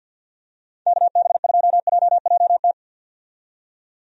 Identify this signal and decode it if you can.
Morse code, heard as a keyed tone, about 47 wpm, 700 Hz. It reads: KB2JJT
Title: USA Call Signs - 6 Characters - 50wpm